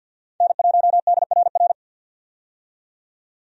Morse code, heard as a keyed tone, about 53 wpm, 700 Hz. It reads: N1DRR
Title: USA Call Signs - 5 Characters - 50wpm